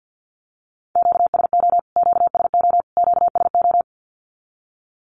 Morse code, heard as a keyed tone, about 50 wpm, 700 Hz. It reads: QSO QSO QSO
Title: Q-Codes Repeated 3x 50wpm